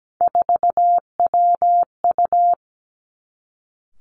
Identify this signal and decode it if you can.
Morse code, heard as a keyed tone, about 17 wpm, 700 Hz. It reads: 4WU
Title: Three Challenging Letter-Number Combinations 17wpm